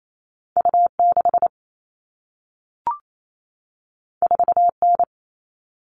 Morse code, heard as a keyed tone, about 28 wpm, 700 Hz. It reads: U6 4N
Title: All Cut Numbers - 2-Digit 28wpm